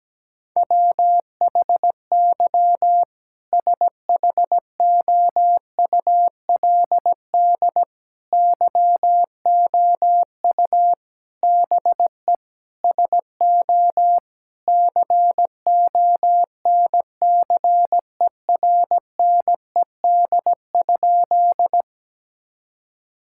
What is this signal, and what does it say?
Morse code, heard as a keyed tone, about 17 wpm, 700 Hz. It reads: WHY SHOULD YOU BE SO CONCERNED?
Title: Sentences from Top 1000 Words - Review Part 6 - 17wpm